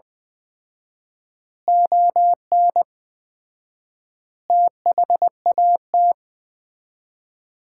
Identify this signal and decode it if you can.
Morse code, heard as a keyed tone, about 20 wpm, 700 Hz. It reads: ON THAT
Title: Common QSO Phrases - 4x Word-Spacing - 20wpm